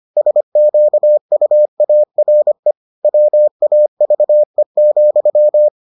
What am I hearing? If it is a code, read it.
Morse code, heard as a keyed tone, about 25 wpm, 600 Hz. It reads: SQUARE WAVE,